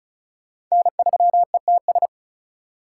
Morse code, heard as a keyed tone, about 36 wpm, 700 Hz. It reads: N3ETS
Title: USA Call Signs - 5 Characters - 3x Character Spacing - 35wpm